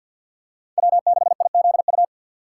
Morse code, heard as a keyed tone, about 51 wpm, 690 Hz. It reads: W6IZU
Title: USA Call Signs - 5 Characters - 6x Character Spacing - 50wpm